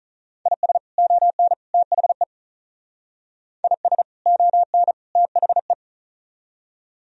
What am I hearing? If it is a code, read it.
Morse code, heard as a keyed tone, about 38 wpm, 700 Hz. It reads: IS ON THE IS ON THE
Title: Speed-Racing - Sets of 3 Words 35wpm